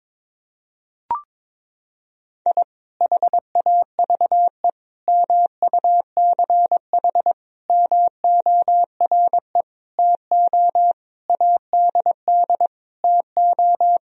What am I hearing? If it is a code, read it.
Morse code, heard as a keyed tone, about 22 wpm, 700 Hz. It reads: I HAVE MUCH MORE TO ADD TO
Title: Sentences from Top 1000 Words - Review Part 1 - 22wpm